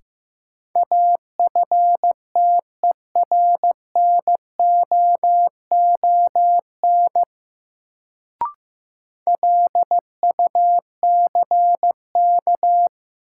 Morse code, heard as a keyed tone, about 15 wpm, 700 Hz. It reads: AFTERNOON LUCK
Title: Sentences from Top 1500 Words - New Word Drill Alt - 15wpm